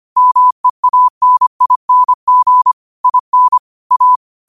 Morse code, heard as a keyed tone, about 25 wpm, 1000 Hz. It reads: MEANING IN A
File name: qotdfiles / 25wpm